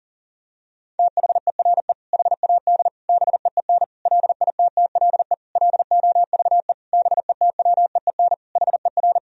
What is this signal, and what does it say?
Morse code, heard as a keyed tone, about 40 wpm, 700 Hz. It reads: THERE HAD BEEN LITTLE LOVE BETWEEN HER
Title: Sentences from Top 1000 Words - Review Part 1 - 40wpm